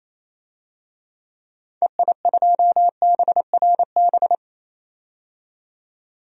Morse code, heard as a keyed tone, about 28 wpm, 700 Hz. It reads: EI2BRB
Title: DXCC-100 Call Signs - 6 Characters - 28wpm